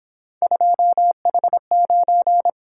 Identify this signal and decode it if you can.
Morse code, heard as a keyed tone, about 27 wpm, 700 Hz. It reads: 2H9
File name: Speed-Racing - Three Letter-Number Combinations - 22wpm